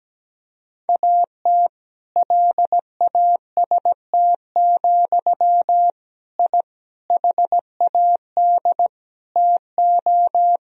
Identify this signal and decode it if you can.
Morse code, heard as a keyed tone, about 17 wpm, 700 Hz. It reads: AT LAST, I HAD TO
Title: Sentences from Top 1000 Words - Review Part 2 - 17wpm